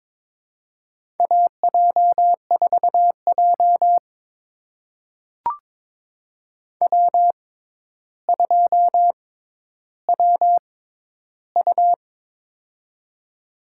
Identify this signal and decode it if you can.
Morse code, heard as a keyed tone, about 22 wpm, spaced as for 11 wpm, 700 Hz. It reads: AJ4J W2WU
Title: USA Call Signs - 4 Characters - 6x Character Spacing - 22wpm